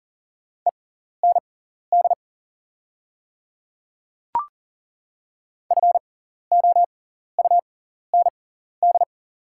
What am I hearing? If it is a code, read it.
Morse code, heard as a keyed tone, about 37 wpm, spaced as for 14 wpm, 690 Hz. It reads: END FOUND
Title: 6x Character Spacing - Top 300 Words - 40wpm